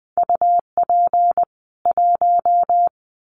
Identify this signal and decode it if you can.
Morse code, heard as a keyed tone, about 20 wpm, 700 Hz. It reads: UP 1
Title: Common QSO Phrases 20wpm